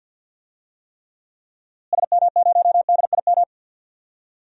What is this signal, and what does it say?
Morse code, heard as a keyed tone, about 53 wpm, 700 Hz. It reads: SM0DIM
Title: DXCC-100 Call Signs - 6 Characters - Repeated 2x - 50wpm